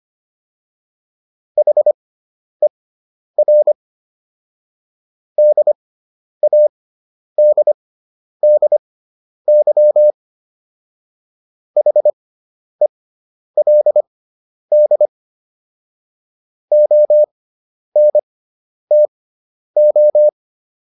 Morse code, heard as a keyed tone, about 25 wpm, spaced as for 10 wpm, 600 Hz. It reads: HER DADDY HELD ONTO